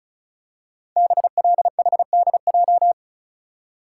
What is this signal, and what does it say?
Morse code, heard as a keyed tone, about 36 wpm, 700 Hz. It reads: BLHDJ
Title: Code Groups - 5 Letters - 35wpm